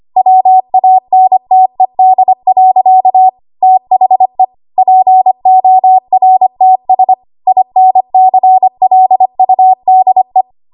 Morse code, heard as a keyed tone, about 25 wpm, 750 Hz. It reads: WANTED. THE PORTS INCLUDE
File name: Archive / 25 WPM / 25WPM